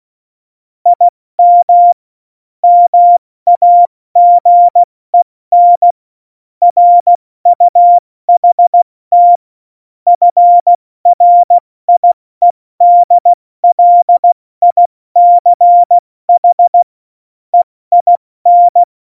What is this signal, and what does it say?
Morse code, heard as a keyed tone, about 16 wpm, 700 Hz. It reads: IM MAGEN RUHT FRIEDLICH EIN
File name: Morsekurs / Witze / Witz 03. November 2020, 70 F80